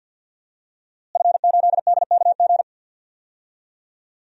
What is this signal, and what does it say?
Morse code, heard as a keyed tone, about 51 wpm, 700 Hz. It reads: F8DKG
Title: DXCC-100 Call Signs - 5 Characters - Repeated 2x - 50wpm